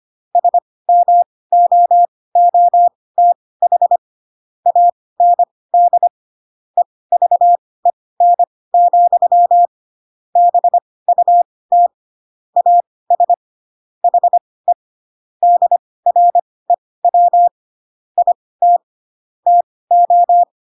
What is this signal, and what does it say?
Morse code, heard as a keyed tone, about 25 wpm, spaced as for 18 wpm, 700 Hz. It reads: SMOOTH AND EVEN, BUT AS HE DREW IT TO
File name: Pinocchio / Pinocchio ch1 12 25 ik1hge